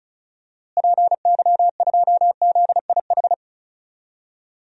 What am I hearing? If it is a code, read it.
Morse code, heard as a keyed tone, about 36 wpm, 700 Hz. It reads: PY2ZIH
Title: DXCC-100 Call Signs - 6 Characters - Repeated 3x - 35wpm